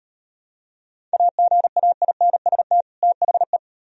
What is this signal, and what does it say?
Morse code, heard as a keyed tone, about 34 wpm, 700 Hz. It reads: AGAINST THE